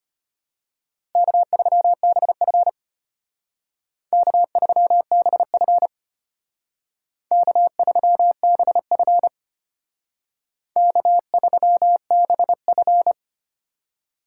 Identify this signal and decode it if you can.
Morse code, heard as a keyed tone, about 34 wpm, 700 Hz. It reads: K3BF K3BF K3BF K3BF